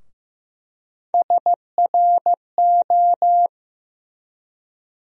Morse code, heard as a keyed tone, about 15 wpm, 700 Hz. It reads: SRO
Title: USA Call Sign Postfixes - 3 Letters 15wpm